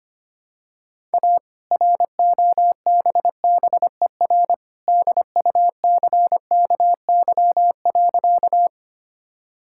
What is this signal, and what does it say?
Morse code, heard as a keyed tone, about 25 wpm, 700 Hz. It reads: A ROBBER DUCKY.
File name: practice / 25wpm / All-Age Appropriate Jokes - 25wpm